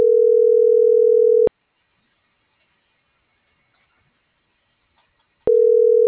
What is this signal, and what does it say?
Ringback tone (North America), 440 + 480 Hz.